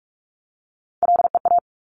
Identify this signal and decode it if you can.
Morse code, heard as a keyed tone, about 46 wpm, 700 Hz. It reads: LEA
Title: Top Three Letter Combinations in Words 45wpm